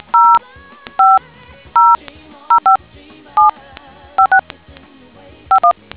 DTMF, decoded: 0500575651